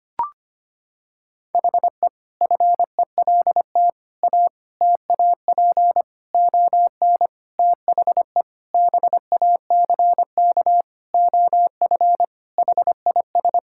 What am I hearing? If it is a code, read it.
Morse code, heard as a keyed tone, about 25 wpm, 700 Hz. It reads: HE FELT A TAP ON THE BACK OF HIS
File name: practice / 25wpm / Sentences with 4-Word Phrases - 25wpm